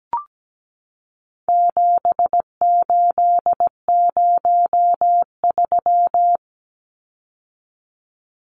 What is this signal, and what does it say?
Morse code, heard as a keyed tone, about 17 wpm, 700 Hz. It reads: 7803
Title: Numbers - 4-Digit 17wpm